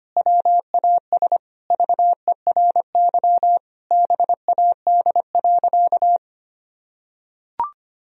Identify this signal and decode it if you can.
Morse code, heard as a keyed tone, about 25 wpm, 700 Hz. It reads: WAS VERY BAD.
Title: Sentences from Top 1000 Words - Review Part 3 - 25wpm